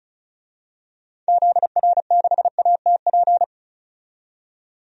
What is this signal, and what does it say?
Morse code, heard as a keyed tone, about 36 wpm, 700 Hz. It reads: ZR6ATP
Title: DXCC-100 Call Signs - 6 Characters - 35wpm